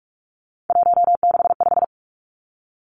Morse code, heard as a keyed tone, about 46 wpm, 700 Hz. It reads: 165
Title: Three Challenging Letter-Number Combinations 45wpm